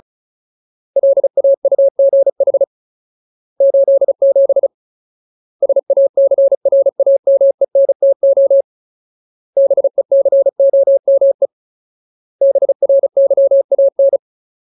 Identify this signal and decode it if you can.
Morse code, heard as a keyed tone, about 34 wpm, 550 Hz. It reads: LAUGH 87 SACRAMENTO BECOME BRYAN
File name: Ultimate Word List / 35WPM Ultimate